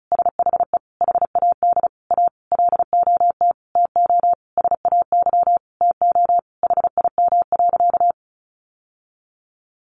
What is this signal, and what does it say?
Morse code, heard as a keyed tone, about 34 wpm, 700 Hz. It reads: SHE HAD A LOT TO SAY TO HIM.
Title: Sentences from Top 1000 Words 35wpm